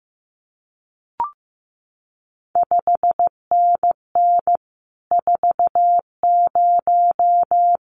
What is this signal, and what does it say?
Morse code, heard as a keyed tone, about 15 wpm, 700 Hz. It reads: 5NN 40
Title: CQ WW DX Contest Exchange Encore - 15wpm